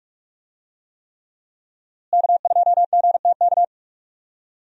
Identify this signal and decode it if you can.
Morse code, heard as a keyed tone, about 46 wpm, 700 Hz. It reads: K2GTK
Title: USA Call Signs - 5 Characters - Repeated 3x - 45wpm